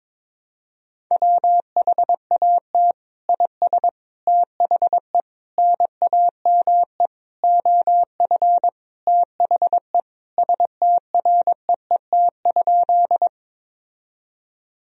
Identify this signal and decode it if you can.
Morse code, heard as a keyed tone, about 22 wpm, 700 Hz. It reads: WHAT IS THE NAME OF THE STREET?
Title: Sentences from Top 1000 Words - Review Part 3 - 22wpm